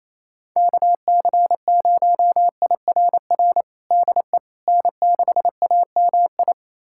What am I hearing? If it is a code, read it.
Morse code, heard as a keyed tone, about 28 wpm, 700 Hz. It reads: KC0IRR DE N6AMI
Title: USA Callsign DE Callsign 28wpm